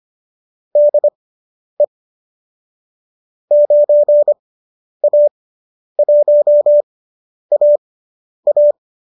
Morse code, heard as a keyed tone, about 25 wpm, spaced as for 10 wpm, 590 Hz. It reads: DE 9A1AA
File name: qso219 10